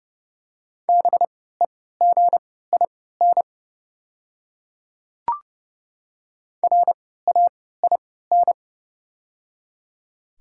Morse code, heard as a keyed tone, about 30 wpm, spaced as for 17 wpm, 690 Hz. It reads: BEGIN RAIN
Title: 3x Character Spacing - Top 300 Words - 30wpm